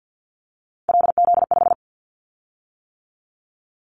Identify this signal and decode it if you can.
Morse code, heard as a keyed tone, about 50 wpm, 700 Hz. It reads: LZ5
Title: Callsign Prefixes for Top 100 DXCC - 50wpm